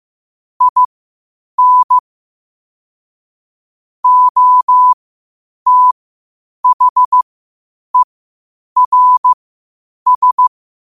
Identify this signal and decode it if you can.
Morse code, heard as a keyed tone, about 15 wpm, spaced as for 8 wpm, 1000 Hz. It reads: IN OTHERS